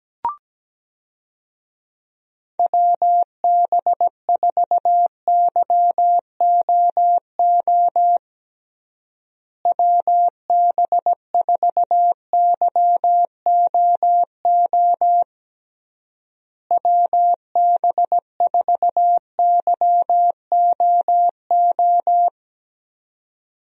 Morse code, heard as a keyed tone, about 17 wpm, 700 Hz. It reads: WB4YOO WB4YOO WB4YOO
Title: USA Call Signs - 6 Characters - Repeated 3x - 17wpm